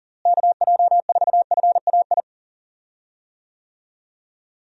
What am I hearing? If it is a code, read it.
Morse code, heard as a keyed tone, about 40 wpm, 700 Hz. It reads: KJ4FAI
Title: USA Call Signs - 6 Characters - Repeated 3x - 40wpm